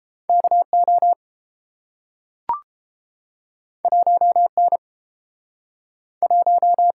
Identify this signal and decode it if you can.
Morse code, heard as a keyed tone, about 30 wpm, 700 Hz. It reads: KO 1N 1